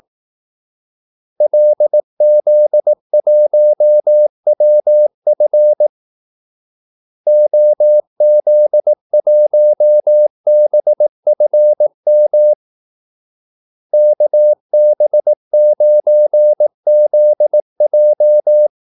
Morse code, heard as a keyed tone, about 18 wpm, 600 Hz. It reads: LZ1WF OZ1BFM KB9ZJ
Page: Kaldesignaler 18 wpm